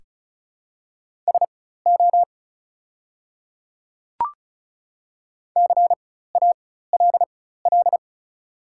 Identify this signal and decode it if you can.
Morse code, heard as a keyed tone, about 34 wpm, spaced as for 16 wpm, 700 Hz. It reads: SO CALL
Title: 4x Character Spacing - Top 200 Words - 35wpm